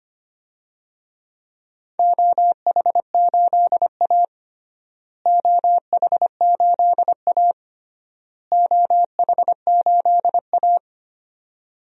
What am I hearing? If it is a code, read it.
Morse code, heard as a keyed tone, about 25 wpm, 700 Hz. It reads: OH8A OH8A OH8A
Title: DXCC-100 Call Signs - 4 Characters - Repeated 3x - 25wpm